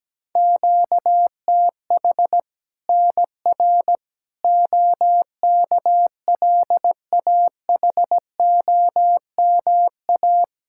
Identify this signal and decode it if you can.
Morse code, heard as a keyed tone, about 17 wpm, 700 Hz. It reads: QTH NR OKLAHOMA
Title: Practice QSO - Exchange 1 - Beginning - 17wpm